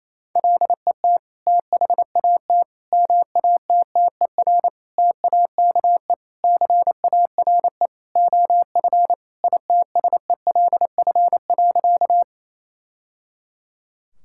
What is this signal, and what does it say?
Morse code, heard as a keyed tone, about 28 wpm, 700 Hz. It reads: LET THAT MATTER TAKE CARE OF ITSELF.
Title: Sentences from Top 1000 Words - Review Part 5 - 28wpm